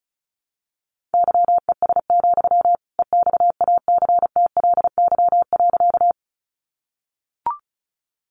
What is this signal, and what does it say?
Morse code, heard as a keyed tone, about 34 wpm, 700 Hz. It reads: YES, EXACTLY.
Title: Sentences from Top 900 Words 35wpm